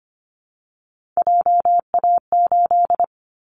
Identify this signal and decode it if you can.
Morse code, heard as a keyed tone, about 24 wpm, 700 Hz. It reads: JA8
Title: Callsign Prefixes for Top 50 DXCC - 25wpm